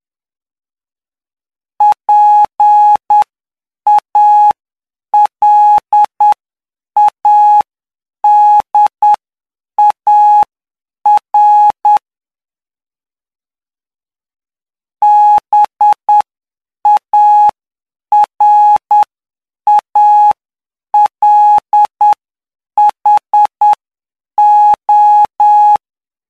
Morse code, heard as a keyed tone, about 9 wpm, spaced as for 7 wpm, 840 Hz. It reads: PALADAR BARALHO
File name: cw / py2lcd / 05-PY2LCD-Li��o 14